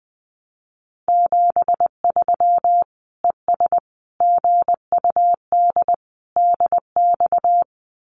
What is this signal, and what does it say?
Morse code, heard as a keyed tone, about 20 wpm, 700 Hz. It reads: 73 ES GUD DX
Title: Common QSO Phrases 20wpm